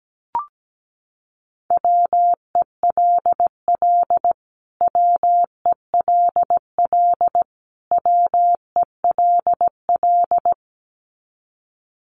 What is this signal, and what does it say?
Morse code, heard as a keyed tone, about 17 wpm, 700 Hz. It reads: WELL WELL WELL
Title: Top 200 Words Repeated 17wpm